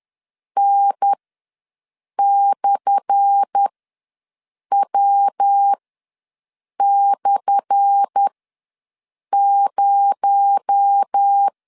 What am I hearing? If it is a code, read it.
Morse code, heard as a keyed tone, about 10 wpm, spaced as for 6 wpm, 800 Hz. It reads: N/W/0
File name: LEZIONI / 10 SIGNS / 10A SLASH ws 10wpm 4121